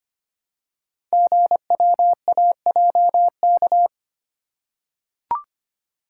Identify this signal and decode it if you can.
Morse code, heard as a keyed tone, about 25 wpm, 700 Hz. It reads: GWAJK
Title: Code Groups - 5 Letter:Numbers - 25wpm